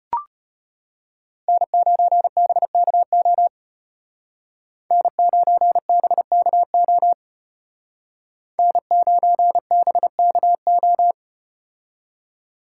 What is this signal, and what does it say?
Morse code, heard as a keyed tone, about 35 wpm, 700 Hz. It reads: N9BKO N9BKO N9BKO